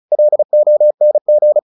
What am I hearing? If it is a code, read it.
Morse code, heard as a keyed tone, about 34 wpm, 600 Hz. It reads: LONG
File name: the old man and the sea / the old man and the sea 1 35wpm